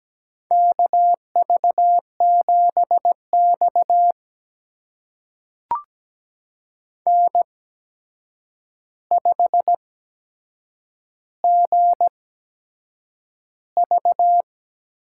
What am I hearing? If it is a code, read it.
Morse code, heard as a keyed tone, about 17 wpm, spaced as for 7 wpm, 700 Hz. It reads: KV7X N5GV
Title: USA Call Signs - 4 Characters - 8x Character Spacing - 17wpm